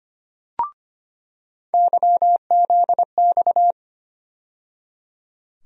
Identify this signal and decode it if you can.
Morse code, heard as a keyed tone, about 25 wpm, 700 Hz. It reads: YZX
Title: USA Call Sign Postfixes - 3 Letters 25wpm